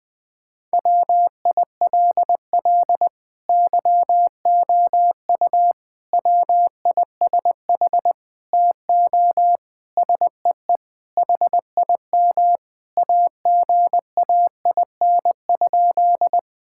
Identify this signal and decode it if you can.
Morse code, heard as a keyed tone, about 20 wpm, 700 Hz. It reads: WILL YOU WISH TO SEE HIM AGAIN?
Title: Sentences from Top 1600 Words 20wpm